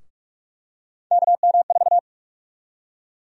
Morse code, heard as a keyed tone, about 42 wpm, 700 Hz. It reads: KM4
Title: USA Call Sign Prefixes - 45wpm